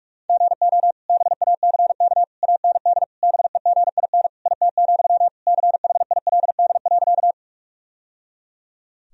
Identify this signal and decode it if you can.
Morse code, heard as a keyed tone, about 46 wpm, 700 Hz. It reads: GO BACK AND BEGIN IT, CHILD.
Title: Sentences from Top 1000 Words - Review Part 1 - 45wpm